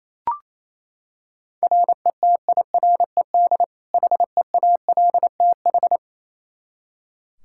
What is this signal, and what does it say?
Morse code, heard as a keyed tone, about 28 wpm, 700 Hz. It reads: RETIRED HEALTH
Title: Practice QSO - Exchange 3 - More Info - 28wpm